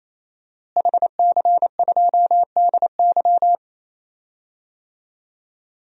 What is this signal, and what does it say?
Morse code, heard as a keyed tone, about 27 wpm, 700 Hz. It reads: HC2DY
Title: DXCC-100 Call Signs - 5 Characters - Repeated 3x - 28wpm